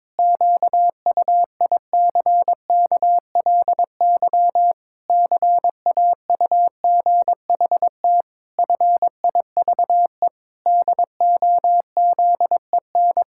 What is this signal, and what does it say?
Morse code, heard as a keyed tone, about 22 wpm, 700 Hz. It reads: QUICKLY CAUGHT FIVE DOZEN
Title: Pangram Challenge v2 - 22wpm